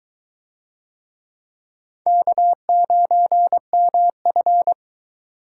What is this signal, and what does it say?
Morse code, heard as a keyed tone, about 21 wpm, 700 Hz. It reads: K9MF